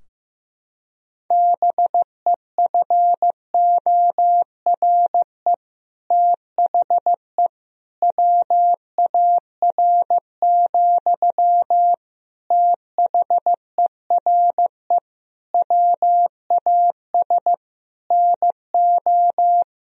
Morse code, heard as a keyed tone, about 15 wpm, 700 Hz. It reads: BEFORE THE WAR, THERE WAS NO
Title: Sentences from Top 1000 Words - Review Part 2 - 15wpm